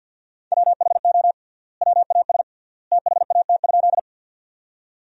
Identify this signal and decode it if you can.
Morse code, heard as a keyed tone, about 50 wpm, 700 Hz. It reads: WHO WAS THAT?
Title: 3x Word-Spacing - Sentences from Top 100 Words - 50wpm